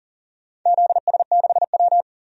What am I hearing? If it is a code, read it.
Morse code, heard as a keyed tone, about 40 wpm, 700 Hz. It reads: ZS6W
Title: DXCC-100 Call Signs - 4 Characters - 40wpm